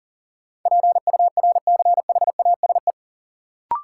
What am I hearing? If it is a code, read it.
Morse code, heard as a keyed tone, about 40 wpm, 700 Hz. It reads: PURCHASE
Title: Sentences from Top 800 Words - New Word Drill - 40wpm